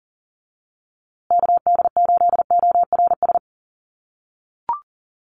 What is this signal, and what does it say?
Morse code, heard as a keyed tone, about 40 wpm, 700 Hz. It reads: KD8ORS
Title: USA Call Signs Repeated 2x 40wpm